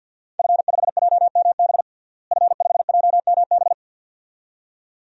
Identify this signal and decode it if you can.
Morse code, heard as a keyed tone, about 52 wpm, 700 Hz. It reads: F5JMB F5JMB
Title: DXCC-100 Call Signs - 5 Characters - Repeated 2x - 50wpm